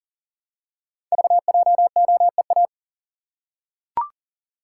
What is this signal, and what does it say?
Morse code, heard as a keyed tone, about 39 wpm, 700 Hz. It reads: VJOEA
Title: Code Groups - 5 Letters - 40wpm